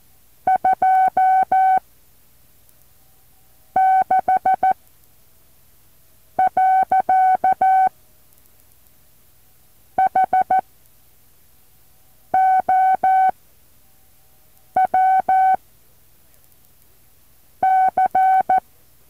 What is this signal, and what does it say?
Morse code, heard as a keyed tone, about 14 wpm, spaced as for 4 wpm, 760 Hz. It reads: 26. HOW C
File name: intl morse code2